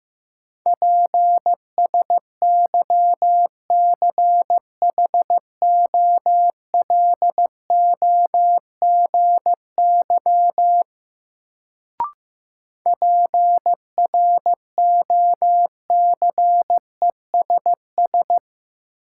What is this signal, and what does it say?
Morse code, heard as a keyed tone, about 15 wpm, 700 Hz. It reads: PSYCHOLOGY PROCESS
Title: Sentences from Top 1400 Words - New Word Drill Alt - 15wpm